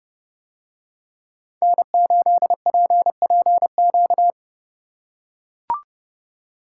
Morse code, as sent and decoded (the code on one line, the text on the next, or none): -. ---.. .--. .--. --.-
N8PPQ